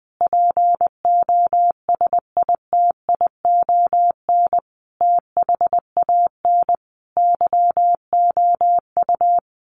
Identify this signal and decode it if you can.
Morse code, heard as a keyed tone, about 20 wpm, 700 Hz. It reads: POSITION THAN YOU
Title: Sentences from Top 1400 Words 20wpm